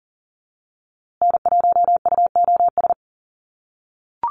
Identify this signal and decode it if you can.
Morse code, heard as a keyed tone, about 40 wpm, 700 Hz. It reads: N1UOS
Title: USA Call Signs Repeated 3x 40wpm